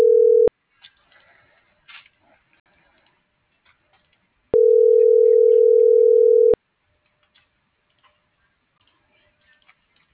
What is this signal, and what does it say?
Ringback tone (North America), 440 + 480 Hz.